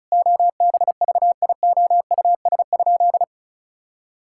Morse code, heard as a keyed tone, about 36 wpm, 700 Hz. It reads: OBVIOUS?
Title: Sentences from Top 1600 Words 35wpm